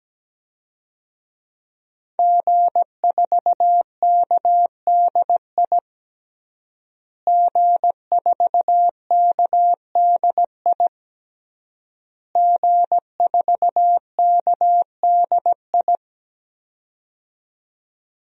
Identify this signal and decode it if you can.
Morse code, heard as a keyed tone, about 17 wpm, 700 Hz. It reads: G4KDI G4KDI G4KDI
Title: DXCC-100 Call Signs - 5 Characters - Repeated 3x - 17wpm